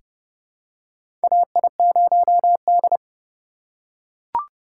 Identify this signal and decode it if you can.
Morse code, heard as a keyed tone, about 30 wpm, 700 Hz. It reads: AI0D
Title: USA Call Signs - 4 Characters - 4x Character Spacing - 30wpm